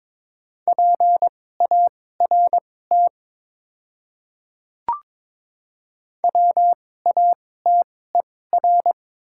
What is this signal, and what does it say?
Morse code, heard as a keyed tone, about 23 wpm, spaced as for 16 wpm, 700 Hz. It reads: PART WATER
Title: 2x Character Spacing - Top 300 Words - 22wpm